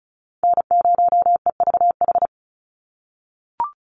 Morse code, heard as a keyed tone, about 36 wpm, 700 Hz. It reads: N0EVH
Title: NA SOTA Chaser Callsigns 35wpm